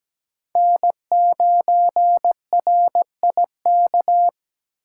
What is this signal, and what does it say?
Morse code, heard as a keyed tone, about 17 wpm, 700 Hz. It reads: N9RIK
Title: USA Call Signs - 5 Characters - 4x Character Spacing - 17wpm